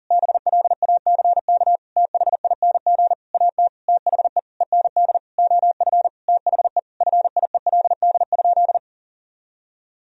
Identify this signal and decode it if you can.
Morse code, heard as a keyed tone, about 40 wpm, 700 Hz. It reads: BLACK THING AT THE END OF THE FIELD?
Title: Sentences from Top 1000 Words - Review Part 2 - 40wpm